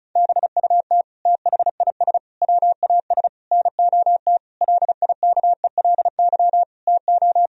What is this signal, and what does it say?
Morse code, heard as a keyed tone, about 36 wpm, 700 Hz. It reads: BUT THIS WAS NOT LIKELY TO
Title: Sentences from Top 1000 Words - Review Part 4 - 35wpm